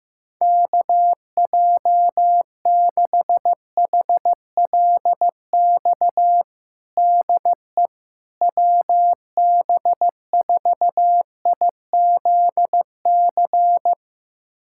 Morse code, heard as a keyed tone, about 15 wpm, 700 Hz. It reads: KJ6HLX DE WB4IZC
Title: USA Callsign DE Callsign - 6 Characters - 15wpm